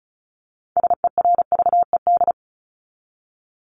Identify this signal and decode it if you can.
Morse code, heard as a keyed tone, about 34 wpm, 700 Hz. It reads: SERVED
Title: ED Suffix - Top 100 Words with Base Words 35wpm